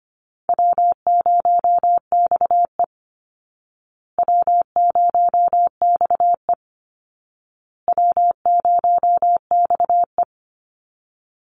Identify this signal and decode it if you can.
Morse code, heard as a keyed tone, about 25 wpm, 700 Hz. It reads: W0XE W0XE W0XE
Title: Easy Call Signs Repeated 3x - 25wpm